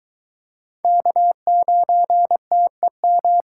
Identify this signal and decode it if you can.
Morse code, heard as a keyed tone, about 23 wpm, 700 Hz. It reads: K9TEM